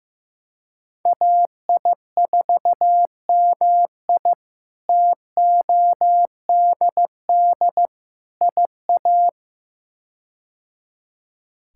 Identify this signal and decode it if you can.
Morse code, heard as a keyed tone, about 15 wpm, 700 Hz. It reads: AI4MI TODD IA
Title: K1USN SST QSO Practice - 15wpm